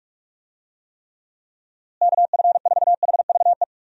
Morse code, heard as a keyed tone, about 41 wpm, 700 Hz. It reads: KF4HVE
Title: USA Call Signs - 6 Characters - Repeated 2x - 45wpm